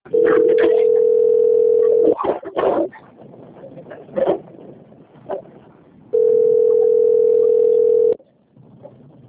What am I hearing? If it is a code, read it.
Ringback tone (North America), 440 + 480 Hz.
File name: out- - - - - .38942